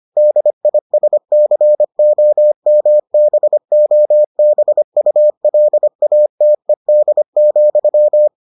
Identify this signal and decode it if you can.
Morse code, heard as a keyed tone, about 25 wpm, 600 Hz. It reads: DISCOMBOBULATED,